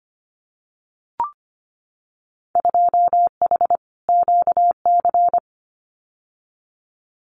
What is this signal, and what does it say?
Morse code, heard as a keyed tone, about 25 wpm, 700 Hz. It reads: 2H QC
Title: Winter Field Day Exchanges 25wpm